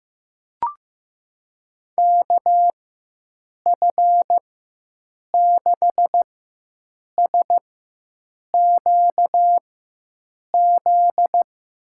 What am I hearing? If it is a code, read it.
Morse code, heard as a keyed tone, about 15 wpm, spaced as for 7 wpm, 700 Hz. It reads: KF6SQZ
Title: USA Call Signs - 6 Characters - 4x Character Spacing - 15wpm